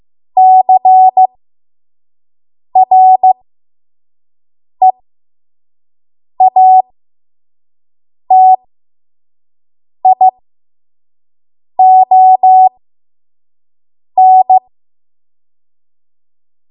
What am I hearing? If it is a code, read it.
Morse code, heard as a keyed tone, about 15 wpm, spaced as for 5 wpm, 750 Hz. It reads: CREATION
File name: Morse / 05WPM